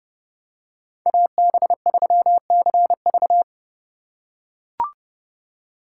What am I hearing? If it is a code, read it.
Morse code, heard as a keyed tone, about 30 wpm, 700 Hz. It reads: AB3CV